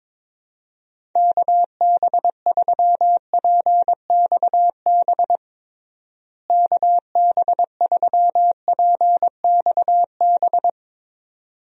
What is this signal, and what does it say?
Morse code, heard as a keyed tone, about 22 wpm, 700 Hz. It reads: KB3PXB KB3PXB
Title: USA Call Signs - 6 Characters - Repeated 2x - 22wpm